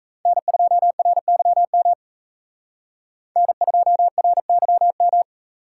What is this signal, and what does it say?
Morse code, heard as a keyed tone, about 34 wpm, 700 Hz. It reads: N2RYM N2RYM
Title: Speed Racing - USA 1x3 Call Signs 28wpm